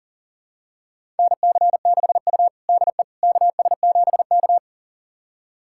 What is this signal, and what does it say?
Morse code, heard as a keyed tone, about 40 wpm, 700 Hz. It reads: NC6U DE KS7K
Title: USA Callsign DE Callsign - 4 Characters - 40wpm